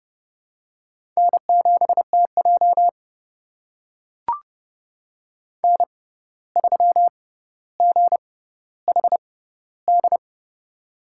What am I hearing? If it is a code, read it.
Morse code, heard as a keyed tone, about 30 wpm, spaced as for 14 wpm, 700 Hz. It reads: N7TJ N3GHD
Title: USA Call Signs - 6x Character Spacing - 30wpm